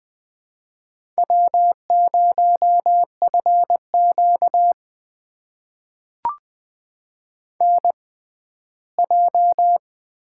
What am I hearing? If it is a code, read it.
Morse code, heard as a keyed tone, about 20 wpm, spaced as for 14 wpm, 700 Hz. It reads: W0FQ NJ
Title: USA Call Signs - 4 Characters - 6x Character Spacing - 20wpm